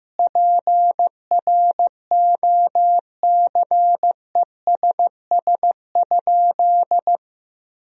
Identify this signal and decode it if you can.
Morse code, heard as a keyed tone, about 15 wpm, 700 Hz. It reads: PROCESS?
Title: Sentences from Top 1000 Words - Review Part 3 - 15wpm